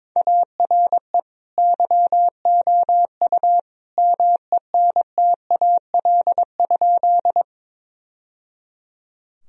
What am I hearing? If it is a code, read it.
Morse code, heard as a keyed tone, about 22 wpm, 700 Hz. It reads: ARE YOU MENTAL?
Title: Sentences from Top 1000 Words - Review Part 4 - 22wpm